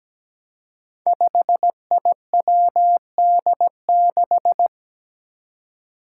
Morse code, heard as a keyed tone, about 17 wpm, 700 Hz. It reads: 5IWD6
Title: Five Challenging Letter-Number Combinations 17wpm